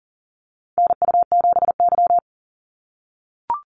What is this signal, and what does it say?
Morse code, heard as a keyed tone, about 40 wpm, 700 Hz. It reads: NU7Y
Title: NA SOTA Activator Callsigns 40wpm